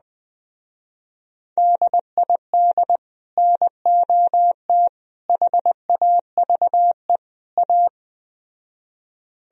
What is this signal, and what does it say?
Morse code, heard as a keyed tone, about 20 wpm, 700 Hz. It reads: DID NOT HAVE A
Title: Speed-Racing - Sets of 4 Words 17wpm